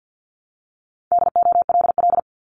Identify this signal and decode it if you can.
Morse code, heard as a keyed tone, about 50 wpm, 700 Hz. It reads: DOLL
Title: Multi-letter Phonemes 50wpm